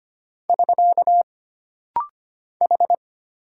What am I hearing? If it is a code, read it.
Morse code, heard as a keyed tone, about 25 wpm, 700 Hz. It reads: <SK> H
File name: Single Character (Letter:Number:Punctuation:Prosign) - Rapid-Fire - 25wpm